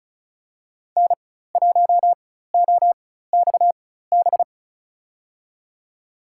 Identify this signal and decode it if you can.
Morse code, heard as a keyed tone, about 36 wpm, spaced as for 16 wpm, 700 Hz. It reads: N1OXB
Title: USA Call Signs - 5 Characters - 4x Character Spacing - 35wpm